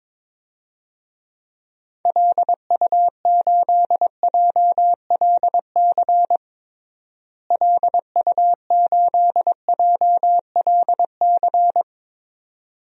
Morse code, heard as a keyed tone, about 22 wpm, 700 Hz. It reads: LU8JLC LU8JLC
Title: DXCC-100 Call Signs - 6 Characters - Repeated 2x - 22wpm